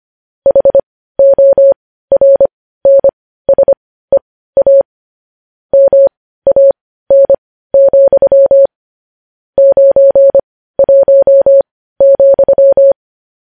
Morse code, heard as a keyed tone, about 25 wpm, spaced as for 15 wpm, 550 Hz. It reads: HORNSEA MAN, 91,
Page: news headlines in morse code at 15 wpm